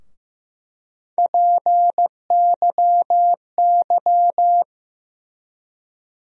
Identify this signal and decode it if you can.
Morse code, heard as a keyed tone, about 15 wpm, 700 Hz. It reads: PYY
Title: USA Call Sign Postfixes - 3 Letters 15wpm